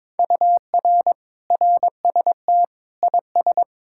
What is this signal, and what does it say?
Morse code, heard as a keyed tone, about 22 wpm, 700 Hz. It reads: UR RST IS
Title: Speed-Racing Common QSO Phrases - 22wpm